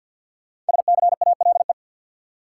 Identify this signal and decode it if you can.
Morse code, heard as a keyed tone, about 45 wpm, 700 Hz. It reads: SCALE
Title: Sentences from Top 1000 Words - Review Part 4 - New Word Drill - 50wpm